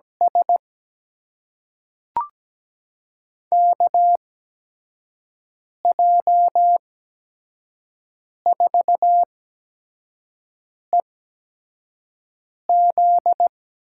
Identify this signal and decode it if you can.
Morse code, heard as a keyed tone, about 17 wpm, spaced as for 5 wpm, 700 Hz. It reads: S KJ4EZ
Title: USA Call Signs - 6 Characters - 8x Character Spacing - 17wpm